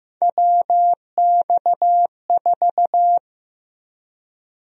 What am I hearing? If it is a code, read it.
Morse code, heard as a keyed tone, about 15 wpm, 700 Hz. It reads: WX4
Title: USA Call Sign Prefixes - Repeated 2x - 15wpm